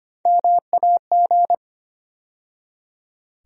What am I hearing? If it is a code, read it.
Morse code, heard as a keyed tone, about 25 wpm, 700 Hz. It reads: MAG
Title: USA Call Sign Postfixes - 3 Letters 25wpm